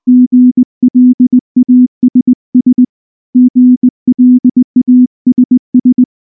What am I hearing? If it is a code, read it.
Morse code, heard as a keyed tone, about 19 wpm, 250 Hz. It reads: GLASS GLASS
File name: morse / morse-training-20WPM-7